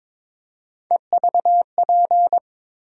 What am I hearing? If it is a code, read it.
Morse code, heard as a keyed tone, about 22 wpm, 700 Hz. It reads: EVP
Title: USA Call Sign Postfixes - 3 Letters 22wpm